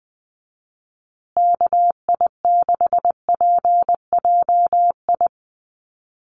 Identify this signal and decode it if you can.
Morse code, heard as a keyed tone, about 20 wpm, 700 Hz. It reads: KI6PJI
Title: USA Call Signs Repeated 3x 20wpm